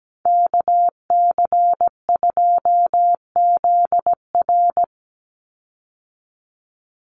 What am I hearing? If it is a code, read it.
Morse code, heard as a keyed tone, about 17 wpm, 700 Hz. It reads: KC2ZR
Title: USA Call Signs 17wpm